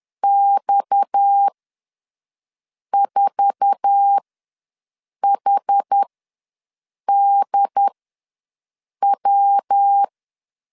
Morse code, heard as a keyed tone, about 10 wpm, spaced as for 5 wpm, 800 Hz. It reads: X4HDW
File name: LEZIONI / 10 SIGNS / 10A SLASH ws 10wpm 4121